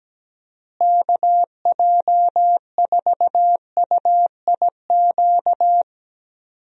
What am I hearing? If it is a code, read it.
Morse code, heard as a keyed tone, about 17 wpm, 700 Hz. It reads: KJ4UIQ
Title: USA Call Signs - 6 Characters - Repeated 2x - 17wpm